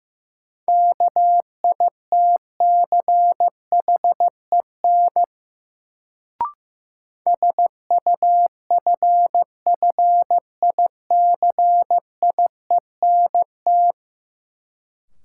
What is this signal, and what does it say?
Morse code, heard as a keyed tone, about 16 wpm, 700 Hz. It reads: KITCHEN SUFFICIENT
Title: Sentences from Top 1400 Words - New Word Drill - 15wpm